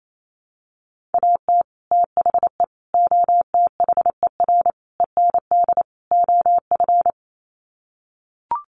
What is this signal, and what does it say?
Morse code, heard as a keyed tone, about 28 wpm, 700 Hz. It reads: AT THE OTHER END OF
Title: Sets of 5 Words 28wpm